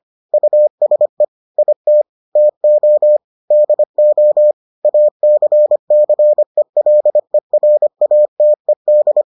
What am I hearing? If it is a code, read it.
Morse code, heard as a keyed tone, about 25 wpm, 600 Hz. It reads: USE IT TO DO ACCELERATED